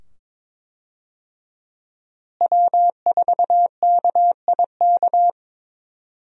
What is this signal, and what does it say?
Morse code, heard as a keyed tone, about 22 wpm, 700 Hz. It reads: W4KIK
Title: USA Call Signs - 5 Characters - Repeated 2x - 22wpm